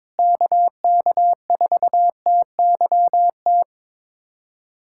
Morse code, heard as a keyed tone, about 21 wpm, 700 Hz. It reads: KK4TYT
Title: USA Callsign DE Callsign - 6 Characters - 22wpm